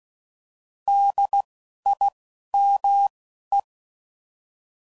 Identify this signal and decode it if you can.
Morse code, heard as a keyed tone, about 16 wpm, spaced as for 11 wpm, 800 Hz. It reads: DIME
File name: Prefixes / By First Letter / Ds